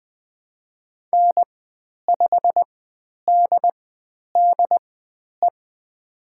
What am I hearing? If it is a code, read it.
Morse code, heard as a keyed tone, about 20 wpm, spaced as for 10 wpm, 700 Hz. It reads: N5DDE
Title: 109 Lesson 18g - Letter m - Call Signs 20wpm